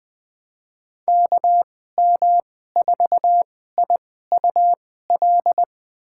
Morse code, heard as a keyed tone, about 20 wpm, spaced as for 14 wpm, 700 Hz. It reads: KM4IUL
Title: USA Call Signs - 2x Character Spacing - 20wpm